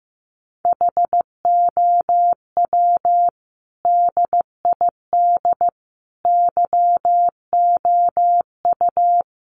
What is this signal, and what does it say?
Morse code, heard as a keyed tone, about 15 wpm, 700 Hz. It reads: HOW DID YOU
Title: Sentences from Top 1100 Words 15wpm